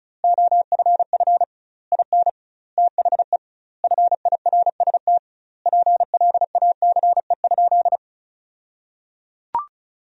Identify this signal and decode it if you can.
Morse code, heard as a keyed tone, about 34 wpm, 700 Hz. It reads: OFF IN THE FIRST PLACE?
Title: 2x Word-Spacing - Sentences from Top 200 Words - 35wpm